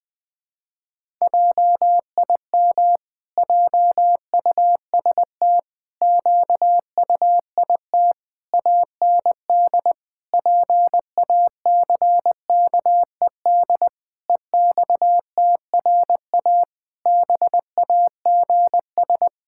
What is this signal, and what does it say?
Morse code, heard as a keyed tone, about 20 wpm, 700 Hz. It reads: JIM JUST QUIT AND PACKED EXTRA BAGS
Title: Pangram Challenge v2 - 20wpm